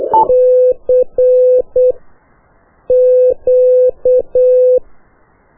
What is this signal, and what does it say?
Morse code, heard as a keyed tone, about 8 wpm, spaced as for 5 wpm, 500 Hz. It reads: CQ
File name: qso1-qrs